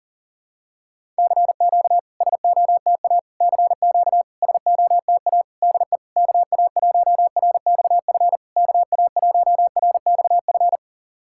Morse code, heard as a keyed tone, about 40 wpm, 690 Hz. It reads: CQ SOTA CQ SOTA DE KA1RXF KA1RXF
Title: SOTA QSO Practice 40wpm